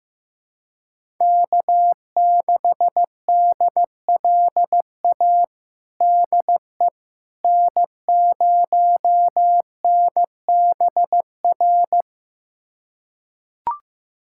Morse code, heard as a keyed tone, about 15 wpm, 700 Hz. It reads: K6DLA DE N0NBR
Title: USA Callsign DE Callsign - 5 Characters - 15wpm